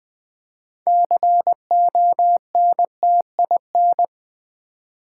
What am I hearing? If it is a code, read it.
Morse code, heard as a keyed tone, about 20 wpm, 700 Hz. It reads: CONTIN
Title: Build A Word - 3 Letters At A Time - 8-Letter Words - 20wpm